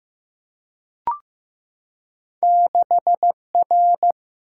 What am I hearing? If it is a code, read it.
Morse code, heard as a keyed tone, about 15 wpm, 700 Hz. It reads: 6R
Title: Single-Letter Single-Number 15wpm